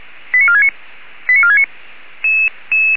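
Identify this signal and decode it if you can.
Five-tone selective call (ZVEI), heard as quick runs of five tones, tones of about 80 ms. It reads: 89469 89469
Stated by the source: ZVEI five-tone selective call 89466 received with an SDR.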